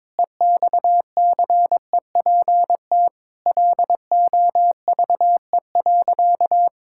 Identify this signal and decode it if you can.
Morse code, heard as a keyed tone, about 21 wpm, 700 Hz. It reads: EXCEPT LOVE.
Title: Sentences from Top 1600 Words 22wpm